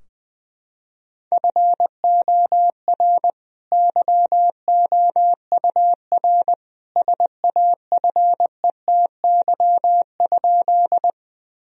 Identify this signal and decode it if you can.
Morse code, heard as a keyed tone, about 21 wpm, 700 Hz. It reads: FOR YOUR SAFETY?
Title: Sentences from Top 1000 Words - Review Part 4 - 20wpm